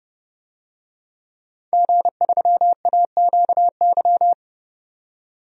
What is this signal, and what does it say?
Morse code, heard as a keyed tone, about 30 wpm, 700 Hz. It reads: G3AQY
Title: DXCC-100 Call Signs - 5 Characters - 30wpm